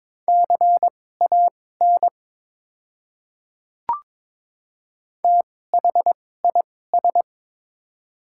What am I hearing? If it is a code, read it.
Morse code, heard as a keyed tone, about 22 wpm, spaced as for 16 wpm, 700 Hz. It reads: CAN THIS
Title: 2x Character Spacing - Top 100 Words - 22wpm